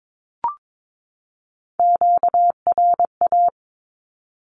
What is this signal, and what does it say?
Morse code, heard as a keyed tone, about 22 wpm, 700 Hz. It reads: QRA
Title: Q-Codes 22wpm